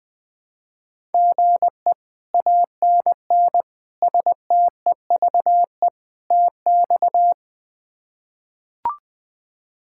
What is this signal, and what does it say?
Morse code, heard as a keyed tone, about 20 wpm, 700 Hz. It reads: GE ANN STEVE TX
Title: K1USN SST QSO Practice - 20wpm